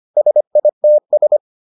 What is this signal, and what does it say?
Morse code, heard as a keyed tone, about 25 wpm, 600 Hz. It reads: SITS